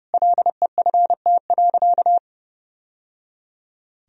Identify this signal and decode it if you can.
Morse code, heard as a keyed tone, about 30 wpm, 700 Hz. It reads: LEFT.
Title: Sentences from Top 1000 Words - Review Part 1 - 30wpm